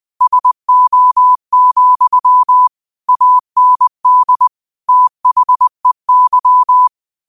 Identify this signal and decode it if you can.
Morse code, heard as a keyed tone, about 20 wpm, 1000 Hz. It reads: SO, AND THEY